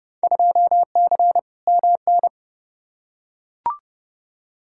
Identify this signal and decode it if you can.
Morse code, heard as a keyed tone, about 31 wpm, 700 Hz. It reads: 2C MN
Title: Field Day Exchanges Encore - 30wpm